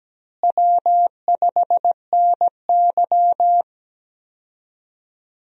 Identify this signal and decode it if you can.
Morse code, heard as a keyed tone, about 17 wpm, 700 Hz. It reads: W5NY
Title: USA Call Signs - 4 Characters - Repeated 3x - 17wpm